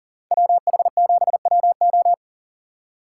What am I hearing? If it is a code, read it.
Morse code, heard as a keyed tone, about 49 wpm, 700 Hz. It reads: WH7WO
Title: Speed Racing - USA 2x2 Call Signs 40wpm